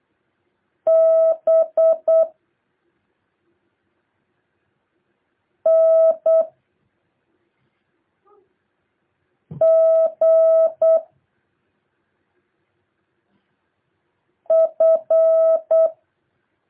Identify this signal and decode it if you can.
Morse code, heard as a keyed tone, about 8 wpm, 650 Hz. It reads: B N G F
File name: jh8cbh / 2023cw